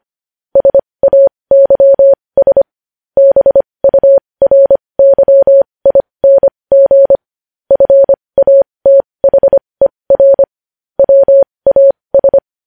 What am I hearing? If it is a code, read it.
Morse code, heard as a keyed tone, about 25 wpm, spaced as for 20 wpm, 550 Hz. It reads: SAYS BURYING FATHER WAS
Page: News Headlines in Morse Code at 20 WPM